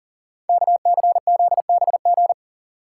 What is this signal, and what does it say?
Morse code, heard as a keyed tone, about 41 wpm, 700 Hz. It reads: KCZBG
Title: Code Groups - 5 Letters - 40wpm